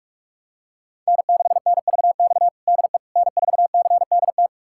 Morse code, heard as a keyed tone, about 42 wpm, 700 Hz. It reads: N6NVX DE N4CDT
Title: USA Callsign DE Callsign - 5 Characters - 45wpm